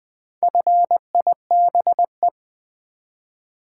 Morse code, heard as a keyed tone, about 20 wpm, 700 Hz. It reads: FIBE
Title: Build A Word - 1 Letter At A Time - 5-Letter Words - 20wpm